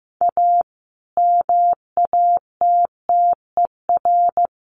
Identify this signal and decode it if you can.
Morse code, heard as a keyed tone, about 14 wpm, 700 Hz. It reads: A MATTER
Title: Sets of 5 Words Encore 15wpm